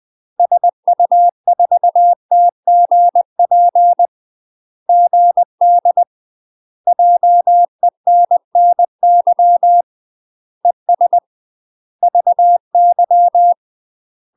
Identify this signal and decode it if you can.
Morse code, heard as a keyed tone, about 20 wpm, 700 Hz. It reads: SU4TGP GD JENNY ES VY
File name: UK MASTER QSOs TRAINING / 20WPM / UK MASTER QSOs 0048